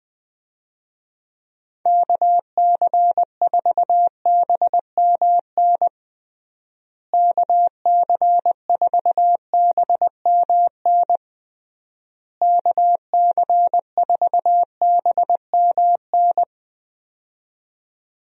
Morse code, heard as a keyed tone, about 20 wpm, 700 Hz. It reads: KC4BMN KC4BMN KC4BMN
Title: USA Call Signs - 6 Characters - Repeated 3x - 20wpm